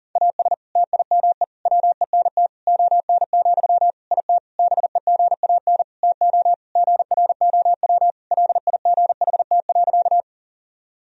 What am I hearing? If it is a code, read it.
Morse code, heard as a keyed tone, about 40 wpm, 700 Hz. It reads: AS TIME WENT ON, IT BEGAN TO GROW LIGHT.
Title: Sentences from Top 1000 Words - Review Part 1 - 40wpm